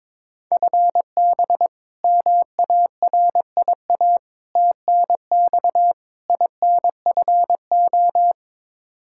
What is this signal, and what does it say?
Morse code, heard as a keyed tone, about 21 wpm, 700 Hz. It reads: FB MARIA TNX INFO
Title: Practice QSO - Exchange 3 - More Info - 22wpm